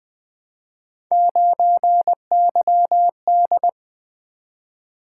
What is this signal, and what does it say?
Morse code, heard as a keyed tone, about 20 wpm, 700 Hz. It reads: 9YD
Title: Three Letter-Number Combinations - 20wpm